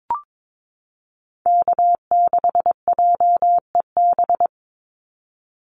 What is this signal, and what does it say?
Morse code, heard as a keyed tone, about 22 wpm, 700 Hz. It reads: K6JEB
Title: Call Signs - 22wpm